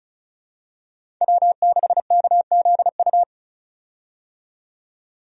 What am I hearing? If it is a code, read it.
Morse code, heard as a keyed tone, about 34 wpm, 700 Hz. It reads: W6KZU
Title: USA Call Signs - 5 Characters - 35wpm